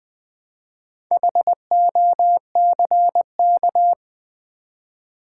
Morse code, heard as a keyed tone, about 20 wpm, 700 Hz. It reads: HOCK
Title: Build A Word - 2 Letters At A Time - 6-Letter Words - 20wpm